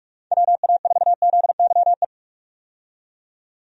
Morse code, heard as a keyed tone, about 44 wpm, 700 Hz. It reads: WA4ZYE
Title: USA Call Signs - 6 Characters - 45wpm